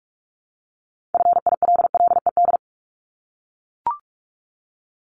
Morse code, heard as a keyed tone, about 44 wpm, 700 Hz. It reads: FILLED
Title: ED Suffix - Top 100 Words with Base Words 45wpm